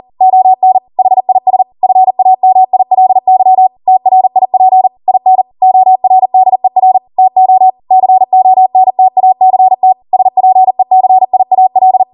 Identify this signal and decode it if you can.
Morse code, heard as a keyed tone, about 40 wpm, 750 Hz. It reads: ON HIS FAMILY TRIP IN ORDER TO CONTACT SPECIAL